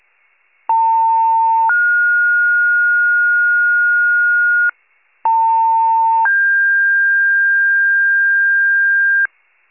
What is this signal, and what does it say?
Two-tone page, heard as a one-second tone, then a three-second tone, about 900 then 1430 Hz, about 900 then 1600 Hz, with noise. It is heard from 0.7 to 9.3 s.
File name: pager / 78 FIRE / wcalert